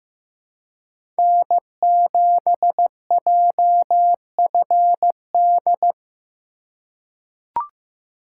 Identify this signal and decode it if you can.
Morse code, heard as a keyed tone, about 15 wpm, 700 Hz. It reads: N7JFD
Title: USA Call Signs - 5 Characters - 2x Character Spacing - 15wpm